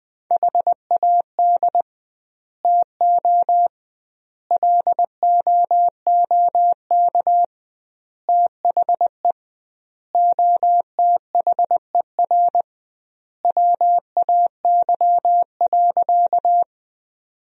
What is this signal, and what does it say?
Morse code, heard as a keyed tone, about 20 wpm, 700 Hz. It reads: HAD TO LOOK THE OTHER WAY.
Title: 2x Word-Spacing - Sentences from Top 100 Words - 20wpm